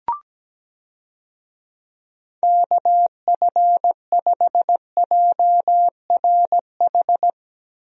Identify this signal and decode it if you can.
Morse code, heard as a keyed tone, about 17 wpm, 700 Hz. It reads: KF5JRH
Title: USA Call Signs - 6 Characters - Repeated 2x - 17wpm